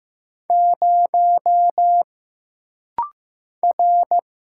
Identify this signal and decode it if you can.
Morse code, heard as a keyed tone, about 15 wpm, 700 Hz. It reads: 0 R
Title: Single Letter-Number Alt - Rapid-Fire 15wpm